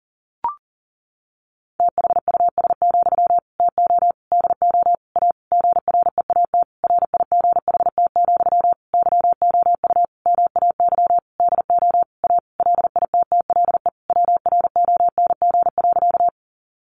Morse code, heard as a keyed tone, about 40 wpm, 700 Hz. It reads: THUS, TO DO A GREAT RIGHT, YOU MAY DO A LITTLE WRONG.
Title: Sentences from Top 900 Words 40wpm